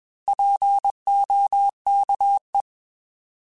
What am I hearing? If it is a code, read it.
Morse code, heard as a keyed tone, about 21 wpm, 800 Hz. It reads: POKE
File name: Alphabetical / Alphabetical mp3s / Ps @20